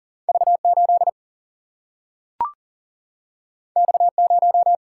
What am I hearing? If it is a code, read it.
Morse code, heard as a keyed tone, about 40 wpm, 700 Hz. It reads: V8 X0
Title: Single-Letter Single-Number 40wpm